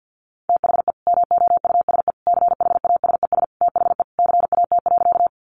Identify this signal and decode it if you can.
Morse code, heard as a keyed tone, about 50 wpm, 700 Hz. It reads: THE MOUSE CHASES THE CAT.
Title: Easy Phrases 50wpm